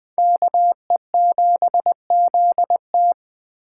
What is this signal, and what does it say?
Morse code, heard as a keyed tone, about 19 wpm, 700 Hz. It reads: KE7ZT